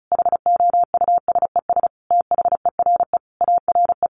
Morse code, heard as a keyed tone, about 34 wpm, 700 Hz. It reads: HOUSES THERE ARE
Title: Easy Phrases 35wpm